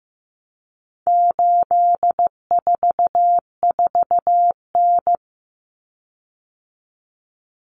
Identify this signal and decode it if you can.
Morse code, heard as a keyed tone, about 15 wpm, 700 Hz. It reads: 844N
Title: Common Cut Numbers - 4-Digit 15wpm